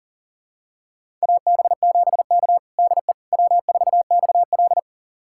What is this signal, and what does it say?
Morse code, heard as a keyed tone, about 40 wpm, 700 Hz. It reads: AB7K DE W4XL
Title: USA Callsign DE Callsign - 4 Characters - 40wpm